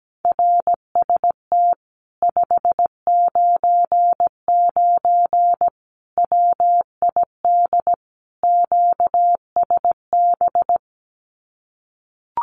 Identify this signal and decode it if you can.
Morse code, heard as a keyed tone, about 17 wpm, 700 Hz. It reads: RST 599 WID QSB
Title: Common QSO Phrases Encore 17wpm